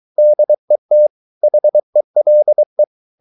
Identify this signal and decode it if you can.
Morse code, heard as a keyed tone, about 23 wpm, 600 Hz. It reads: DET HELE
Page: Ferie 23 wpm